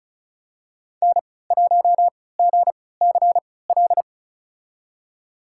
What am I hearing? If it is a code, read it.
Morse code, heard as a keyed tone, about 36 wpm, spaced as for 20 wpm, 700 Hz. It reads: N1GCL
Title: USA Call Signs - 5 Characters - 3x Character Spacing - 35wpm